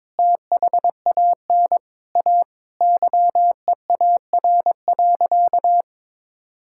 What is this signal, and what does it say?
Morse code, heard as a keyed tone, about 22 wpm, 700 Hz. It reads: THAN A YEAR.
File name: practice / 22wpm / Sentences with 4-Word Phrases - 22wpm